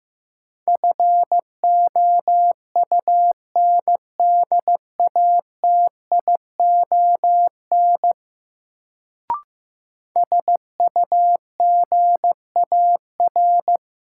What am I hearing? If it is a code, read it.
Morse code, heard as a keyed tone, about 15 wpm, 700 Hz. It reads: FOUNDATION SUGAR
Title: Sentences from Top 1200 Words - New Word Drill - 15wpm